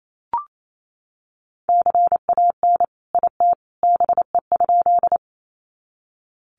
Morse code, heard as a keyed tone, about 28 wpm, 700 Hz. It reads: CAN IT BE?
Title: Sentences from Top 100 Words 28wpm